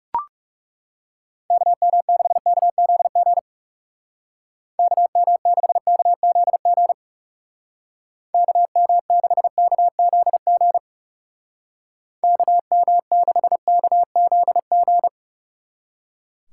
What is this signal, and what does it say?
Morse code, heard as a keyed tone, about 37 wpm, 700 Hz. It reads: KM6KZG KM6KZG KM6KZG KM6KZG
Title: Speed Racing - USA 2x3 Call Signs 30wpm